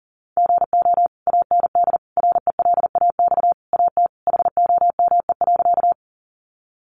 Morse code, heard as a keyed tone, about 40 wpm, 700 Hz. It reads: GO AND RELAX AT HOME.
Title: Sentences from Top 1400 Words 40wpm